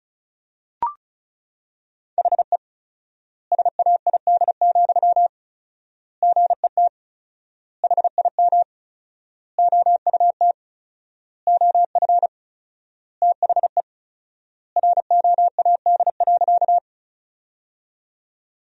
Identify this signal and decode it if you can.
Morse code, heard as a keyed tone, about 36 wpm, 700 Hz. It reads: HE SAID, GET HIM OUT OF THE ROAD.
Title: 4x Word-Spacing - Sentences from Top 400 Words - 35wpm